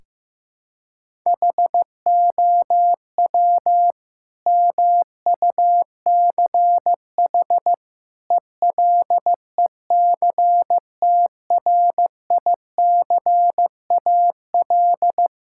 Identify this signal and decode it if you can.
Morse code, heard as a keyed tone, about 15 wpm, 700 Hz. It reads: HOW MUCH ELECTRICAL
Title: Sentences from Top 1000 Words - Review Part 6 - 15wpm